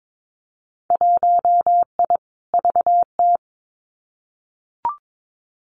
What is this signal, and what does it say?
Morse code, heard as a keyed tone, about 22 wpm, 700 Hz. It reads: 1I VT
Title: Winter Field Day Exchanges Encore 22wpm